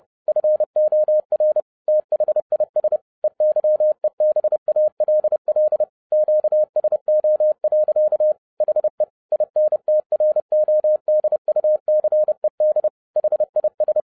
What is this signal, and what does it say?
Morse code, heard as a keyed tone, about 30 wpm, 610 Hz. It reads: FOR THIS EYEBALL QSO. HE INTRODUCED HIS